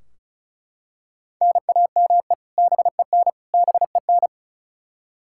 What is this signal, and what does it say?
Morse code, heard as a keyed tone, about 36 wpm, 700 Hz. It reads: NAME BEN BEN
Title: K1USN SST QSO Practice 35wpm